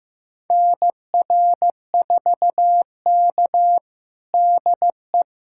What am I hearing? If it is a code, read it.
Morse code, heard as a keyed tone, about 16 wpm, 700 Hz. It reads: NR4K DE
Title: USA Callsign DE Callsign - 4 Characters - 15wpm